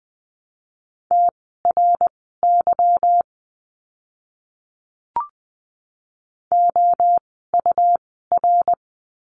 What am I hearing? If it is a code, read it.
Morse code, heard as a keyed tone, about 20 wpm, spaced as for 15 wpm, 700 Hz. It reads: TRY OUR
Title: 160 Lesson 26f - Letter g - Words 20wpm